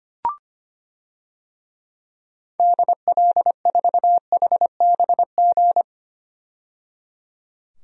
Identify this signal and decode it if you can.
Morse code, heard as a keyed tone, about 25 wpm, 700 Hz. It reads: DL4HBG
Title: DXCC-100 Call Signs - 6 Characters - 25wpm